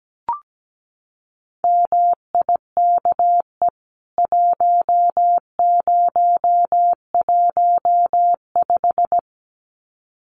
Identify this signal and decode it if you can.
Morse code, heard as a keyed tone, about 17 wpm, 700 Hz. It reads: MIKE 1015
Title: CWT Exchange 17wpm